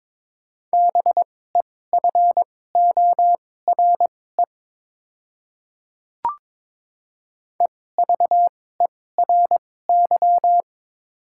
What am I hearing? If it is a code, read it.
Morse code, heard as a keyed tone, about 22 wpm, spaced as for 16 wpm, 700 Hz. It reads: BEFORE EVERY
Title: 2x Character Spacing - Top 200 Words - 22wpm